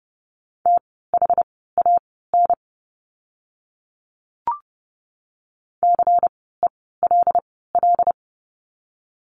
Morse code, heard as a keyed tone, about 32 wpm, spaced as for 17 wpm, 700 Hz. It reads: THAN CELL
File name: Lesson 16f - Letter c - Words 30wpm